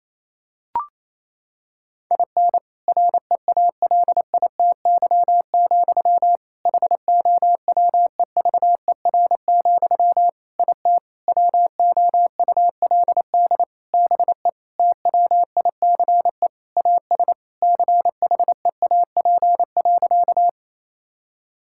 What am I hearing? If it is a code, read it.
Morse code, heard as a keyed tone, about 28 wpm, 700 Hz. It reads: IN REALITY, HOWEVER, IT WOULD BE TWICE AS CHEAP.
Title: Sentences from Top 1600 Words 28wpm